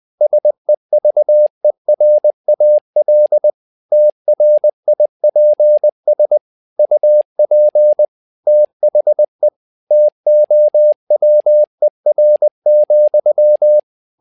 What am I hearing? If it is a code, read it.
Morse code, heard as a keyed tone, about 20 wpm, 600 Hz. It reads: SEVERAL TRIPS UP THE TOWER,
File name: tower trouble 20wpm